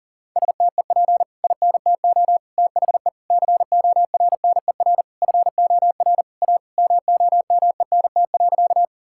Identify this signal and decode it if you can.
Morse code, heard as a keyed tone, about 40 wpm, 700 Hz. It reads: STEP INTO THE CORNER FOR A MOMENT.
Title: Sentences from Top 1600 Words 40wpm